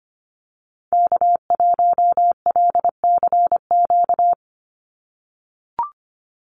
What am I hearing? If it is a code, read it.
Morse code, heard as a keyed tone, about 25 wpm, 700 Hz. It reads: K1LCQ
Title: USA Call Signs 25wpm